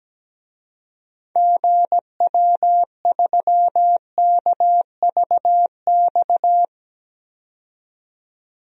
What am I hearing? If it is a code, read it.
Morse code, heard as a keyed tone, about 17 wpm, 700 Hz. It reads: GW3KVX
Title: DXCC-100 Call Signs - 6 Characters - 17wpm